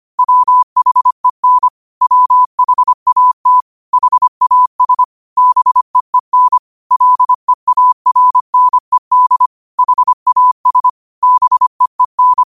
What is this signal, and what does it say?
Morse code, heard as a keyed tone, about 25 wpm, 1000 Hz. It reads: WHEN WHAT HAS BEEN LEARNED HAS BEEN
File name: qotdfiles / 25wpm